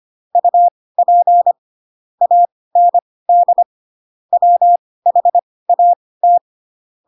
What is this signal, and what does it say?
Morse code, heard as a keyed tone, about 25 wpm, spaced as for 18 wpm, 700 Hz. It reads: UP AND WHAT
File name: Pinocchio ch3 12 25 ik1hge